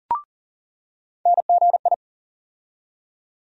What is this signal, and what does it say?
Morse code, heard as a keyed tone, about 40 wpm, 700 Hz. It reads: NGI
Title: USA Call Sign Postfixes - 3 Letters 40wpm